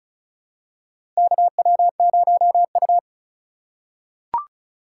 Morse code, heard as a keyed tone, about 36 wpm, 700 Hz. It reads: KW0U
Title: USA Call Signs - 3x Character Spacing - 35wpm